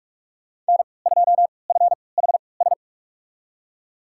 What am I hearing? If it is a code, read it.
Morse code, heard as a keyed tone, about 46 wpm, spaced as for 25 wpm, 700 Hz. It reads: N2FHS
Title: USA Call Signs - 3x Character Spacing - 45wpm